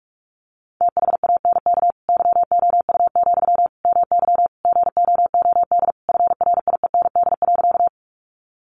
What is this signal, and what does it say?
Morse code, heard as a keyed tone, about 44 wpm, 700 Hz. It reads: THANK YOU, MY GOOD FRIEND.
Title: Sentences from Top 600 Words 45wpm